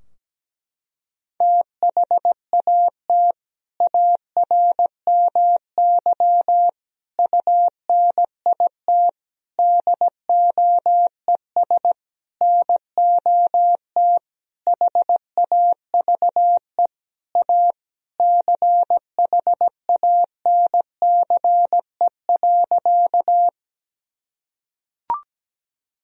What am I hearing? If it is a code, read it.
Morse code, heard as a keyed tone, about 17 wpm, 700 Hz. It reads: THAT ARMY UNIT DOES NOT HAVE A CHANCE.
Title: Sentences from Top 1000 Words - Review Part 2 - 17wpm